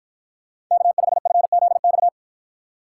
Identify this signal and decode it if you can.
Morse code, heard as a keyed tone, about 53 wpm, 700 Hz. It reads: K5FZX